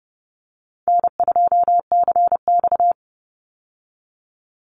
Morse code, heard as a keyed tone, about 30 wpm, 700 Hz. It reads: N2CX
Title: Call Signs Repeated 2x - 30wpm